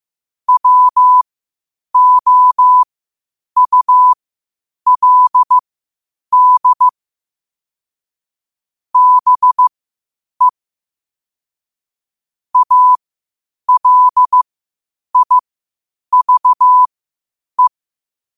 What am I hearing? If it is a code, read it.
Morse code, heard as a keyed tone, about 15 wpm, spaced as for 8 wpm, 1000 Hz. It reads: WOULD BE ALIVE
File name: qotdfiles / 8wpm